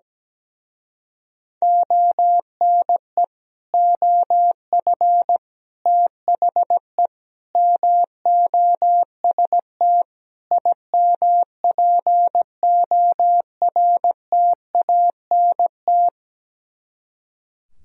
Morse code, heard as a keyed tone, about 17 wpm, 700 Hz. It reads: ONE OF THE MOST IMPORTANT
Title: Speed-Racing - Sets of 5 Words 17wpm